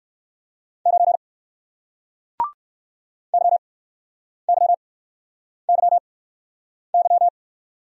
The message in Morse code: 9 YYYY

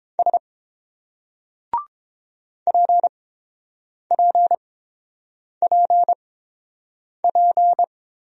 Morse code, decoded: S PPPP

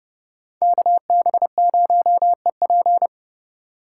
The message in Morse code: KB0EP

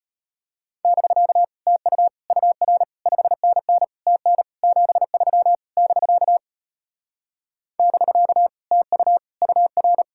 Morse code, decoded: <BK> TU UR 5NN TN 73 <BK> <BK> TU UR

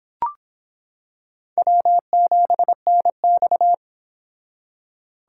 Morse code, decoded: W7NX

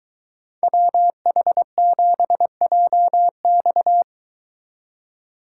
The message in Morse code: WH7JX

